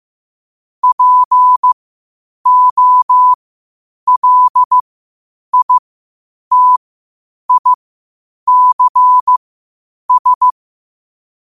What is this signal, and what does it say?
Morse code, heard as a keyed tone, about 15 wpm, spaced as for 8 wpm, 1000 Hz. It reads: POLITICS